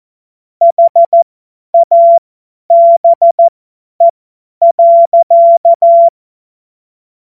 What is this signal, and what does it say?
Morse code, heard as a keyed tone, about 14 wpm, spaced as for 10 wpm, 680 Hz. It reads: HABE.